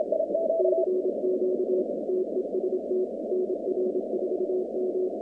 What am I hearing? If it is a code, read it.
Morse code, heard as a keyed tone, about 27 wpm, 370 Hz, with noise. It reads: CQ TEST NR4M
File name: contests / 2019-cw-wpx-kt5j-k5pi- -2330 0-47